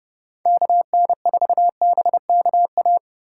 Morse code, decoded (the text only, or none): KN4BKA